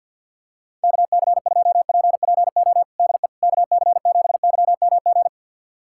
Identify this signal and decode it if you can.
Morse code, heard as a keyed tone, about 52 wpm, 700 Hz. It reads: KC2PPO DE KC7YMG
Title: USA Callsign DE Callsign - 6 Characters - 50wpm